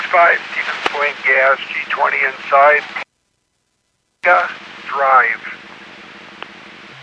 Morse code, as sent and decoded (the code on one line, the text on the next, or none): none